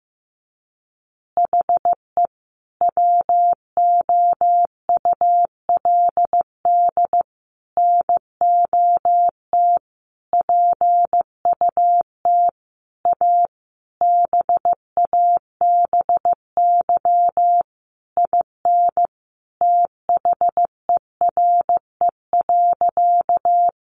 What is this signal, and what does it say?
Morse code, heard as a keyed tone, about 15 wpm, 700 Hz. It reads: HE WOULD NOT PUT A BABY IN THERE.
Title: Easy Phrases 15wpm